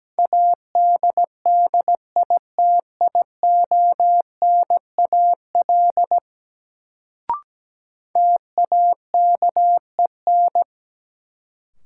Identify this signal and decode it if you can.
Morse code, heard as a keyed tone, about 17 wpm, 700 Hz. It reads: ADDITIONAL TAKEN
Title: Sentences from Top 1000 Words - Review Part 4 - New Word Drill - 17wpm